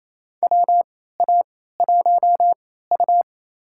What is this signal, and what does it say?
Morse code, heard as a keyed tone, about 28 wpm, spaced as for 16 wpm, 700 Hz. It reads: WA1U
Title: USA Call Signs - 4 Characters - 3x Character Spacing - 28wpm